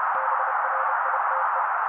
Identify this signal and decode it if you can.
Morse code, heard as a keyed tone, about 30 wpm, 530 Hz. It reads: BRITE
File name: daily / 29wpm noise